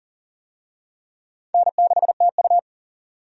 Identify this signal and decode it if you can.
Morse code, heard as a keyed tone, about 40 wpm, 700 Hz. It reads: N6TU
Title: USA Call Signs - 4 Characters - 40wpm